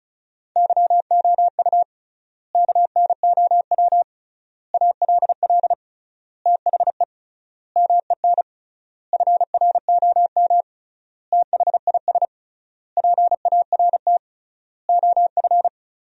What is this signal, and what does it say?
Morse code, heard as a keyed tone, about 36 wpm, 700 Hz. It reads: YOU KNOW ALL THE MEN FROM THIS PART OF
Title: 3x Word-Spacing - Sentences from Top 300 Words - 35wpm